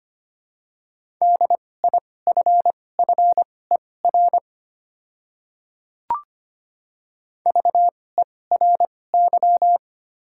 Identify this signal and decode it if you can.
Morse code, heard as a keyed tone, about 25 wpm, spaced as for 18 wpm, 700 Hz. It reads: DIFFER VERY
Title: 2x Character Spacing - Top 200 Words - 25wpm